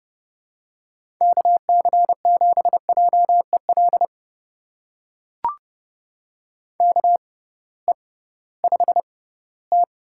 Morse code, heard as a keyed tone, about 30 wpm, spaced as for 18 wpm, 700 Hz. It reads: KC7JEL KE5T
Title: USA Call Signs - 6 Characters - 6x Character Spacing - 30wpm